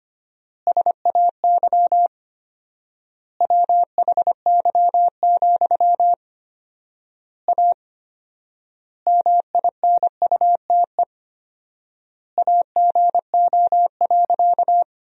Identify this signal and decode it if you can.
Morse code, heard as a keyed tone, about 25 wpm, 700 Hz. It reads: SAY WHY, A MINUTE AGO.
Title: 4x Word-Spacing - Sentences from Top 500 Words - 25wpm